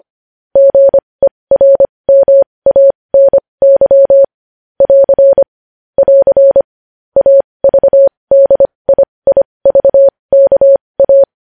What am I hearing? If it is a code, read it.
Morse code, heard as a keyed tone, about 25 wpm, spaced as for 20 wpm, 550 Hz. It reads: GERMANY <AR> <AR> AVDIIVKA
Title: News Headlines at 20